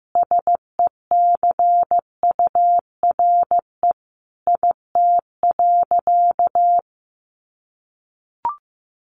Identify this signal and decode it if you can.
Morse code, heard as a keyed tone, about 14 wpm, 700 Hz. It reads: SECURE IT.
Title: Sentences from Top 1200 Words 15wpm